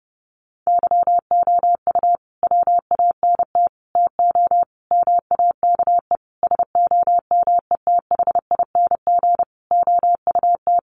Morse code, heard as a keyed tone, about 30 wpm, 700 Hz. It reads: YOU WANT TO MAKE SOMETHING OUT
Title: Sentences from Top 600 Words 30wpm